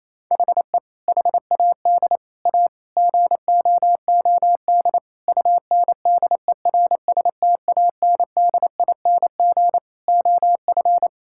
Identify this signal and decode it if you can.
Morse code, heard as a keyed tone, about 28 wpm, 700 Hz. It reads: HE HAD A GOOD UNDERSTANDING OF
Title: Sentences from Top 1000 Words - Review Part 6 - 28wpm